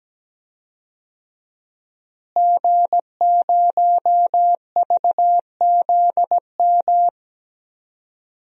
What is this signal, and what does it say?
Morse code, heard as a keyed tone, about 17 wpm, 700 Hz. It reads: G0VZM
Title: DXCC-100 Call Signs - 5 Characters - Repeated 3x - 17wpm